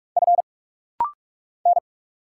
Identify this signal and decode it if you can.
Morse code, heard as a keyed tone, about 44 wpm, 700 Hz. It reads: F N
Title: Single Letter-Number - Rapid-Fire 45wpm